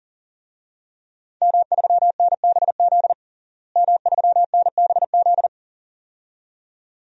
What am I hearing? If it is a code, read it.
Morse code, heard as a keyed tone, about 40 wpm, 700 Hz. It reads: M3NBZ M3NBZ
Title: DXCC-100 Call Signs - 5 Characters - Repeated 2x - 40wpm